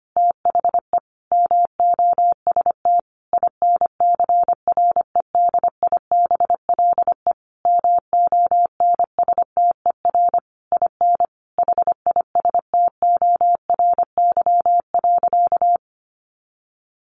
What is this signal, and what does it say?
Morse code, heard as a keyed tone, about 25 wpm, 700 Hz. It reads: THE MOST INCREDIBLE MONSTER IN HISTORY.
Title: A Sound Of Thunder 25wpm